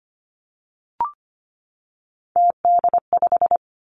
Morse code, heard as a keyed tone, about 25 wpm, 700 Hz. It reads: TD5
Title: Three Challenging Letter-Number Combinations 25wpm